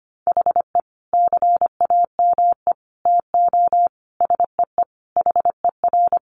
Morse code, heard as a keyed tone, about 25 wpm, 700 Hz. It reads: HE CAME TO SEE HER
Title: Sentences from Top 1300 Words 25wpm